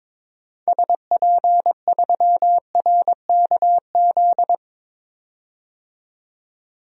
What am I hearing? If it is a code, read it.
Morse code, heard as a keyed tone, about 22 wpm, 700 Hz. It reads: SP3RKZ
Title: DXCC-100 Call Signs - 6 Characters - 22wpm